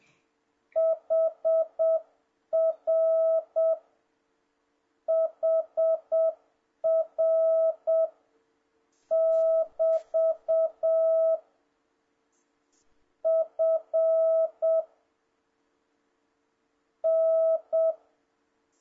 Morse code, heard as a keyed tone, about 7 wpm, spaced as for 6 wpm, 650 Hz. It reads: HR HR<BT> F N
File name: jh8cbh / 2026cw